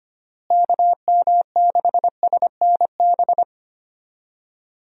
Morse code, heard as a keyed tone, about 25 wpm, 700 Hz. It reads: KM6SNB
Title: USA Call Signs - 6 Characters - Repeated 2x - 25wpm